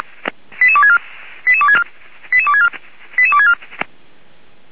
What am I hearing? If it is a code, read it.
Five-tone selective call (ZVEI), heard as quick runs of five tones, tones of about 70 ms. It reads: 88264 88264 88264 88264
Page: /464/SDR/ZVEI/88153/